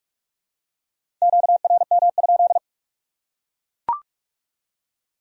Morse code, heard as a keyed tone, about 38 wpm, 700 Hz. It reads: QRM?